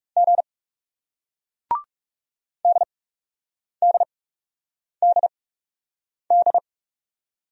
Morse code, heard as a keyed tone, about 39 wpm, spaced as for 8 wpm, 700 Hz. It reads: G DDDD